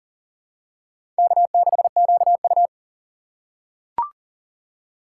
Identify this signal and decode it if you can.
Morse code, heard as a keyed tone, about 39 wpm, 700 Hz. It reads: K6QU
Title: USA Call Signs - 4 Characters - 3x Character Spacing - 40wpm